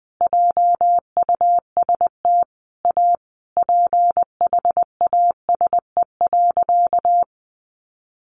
Morse code, heard as a keyed tone, about 20 wpm, 700 Hz. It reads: JUST A PHASE.
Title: Sentences from Top 1200 Words 20wpm